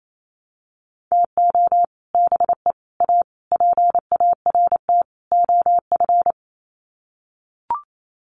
Morse code, heard as a keyed tone, about 28 wpm, 700 Hz. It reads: TO BE A PART OF
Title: Sets of 5 Words 28wpm